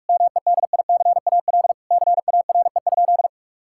Morse code, heard as a keyed tone, about 41 wpm, 700 Hz. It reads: MEDICAL CARE?
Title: Sentences from Top 1000 Words - Review Part 5 - 45wpm